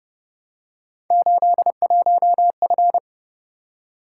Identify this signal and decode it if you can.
Morse code, heard as a keyed tone, about 29 wpm, 700 Hz. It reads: 81F
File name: Speed-Racing - Three Letter-Number Combinations - 25wpm